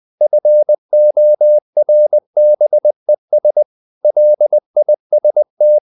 Morse code, heard as a keyed tone, about 20 wpm, 600 Hz. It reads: FORBES LIST
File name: CW / Practice / EnglishQuotations0320